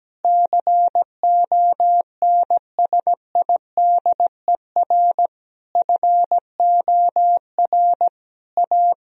Morse code, heard as a keyed tone, about 17 wpm, 700 Hz. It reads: CONSIDER FOR A
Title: Sentences from Top 1000 Words - Review Part 4 - 17wpm